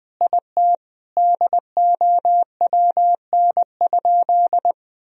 Morse code, heard as a keyed tone, about 21 wpm, 700 Hz. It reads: IT DOWN?
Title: Sentences from Top 1000 Words - Review Part 1 - 20wpm